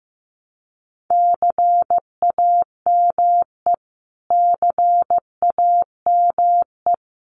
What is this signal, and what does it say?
Morse code, heard as a keyed tone, about 15 wpm, 700 Hz. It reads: CAME CAME
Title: Top 300 Words Repeated 15wpm